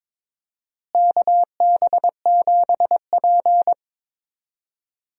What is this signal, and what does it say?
Morse code, heard as a keyed tone, about 22 wpm, 700 Hz. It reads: KB7P
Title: USA Call Signs - 4 Characters - Repeated 3x - 22wpm